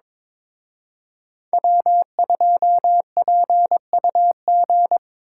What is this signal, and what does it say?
Morse code, heard as a keyed tone, about 26 wpm, 700 Hz. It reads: W2PUG
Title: Speed Racing - USA Call Signs 22wpm